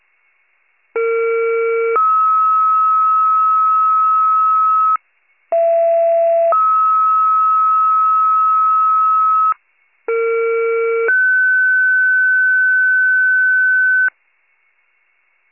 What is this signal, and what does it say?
Two-tone page, heard as a one-second tone, then a three-second tone, about 460 then 1290 Hz, about 670 then 1290 Hz, about 460 then 1500 Hz, with noise. It is heard from 1.0 to 14.1 s.